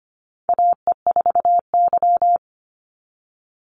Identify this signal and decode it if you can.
Morse code, heard as a keyed tone, about 25 wpm, 700 Hz. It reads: AE4Y
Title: NA SOTA Chaser Callsigns 25wpm